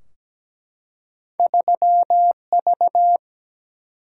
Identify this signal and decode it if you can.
Morse code, heard as a keyed tone, about 17 wpm, 700 Hz. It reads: 3V
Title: Single-Letter Single-Number 17wpm